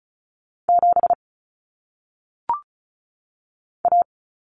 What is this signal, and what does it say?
Morse code, heard as a keyed tone, about 36 wpm, 700 Hz. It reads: 7 A
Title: Common Cut Numbers - 1-Digit 35wpm